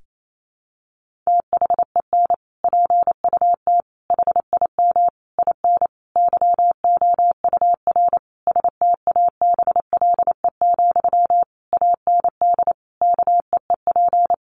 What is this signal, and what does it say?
Morse code, heard as a keyed tone, about 28 wpm, 700 Hz. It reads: THEN PUT HIM IN YOUR STABLE, AND KEEP
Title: Sentences from Top 1500 Words 28wpm